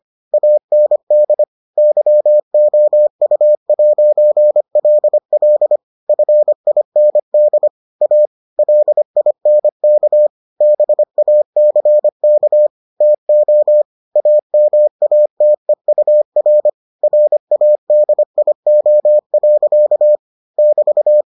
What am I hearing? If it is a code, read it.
Morse code, heard as a keyed tone, about 25 wpm, 600 Hz. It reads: AND YOU'LL FIND A LINK BACK TO AMATEUR RADIO. <BT>